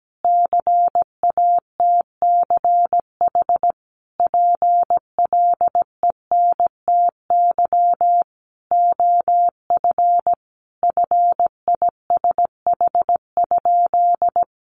Morse code, heard as a keyed tone, about 17 wpm, 700 Hz. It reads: CATCH PLENTY OF FISH?
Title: Sentences from Top 1500 Words 17wpm